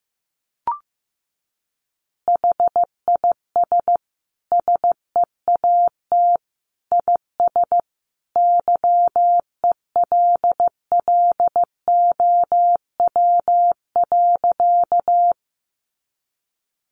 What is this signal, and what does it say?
Morse code, heard as a keyed tone, about 14 wpm, 700 Hz. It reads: HIS SEAT IS YELLOW.
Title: Additional Easy Phrases 15wpm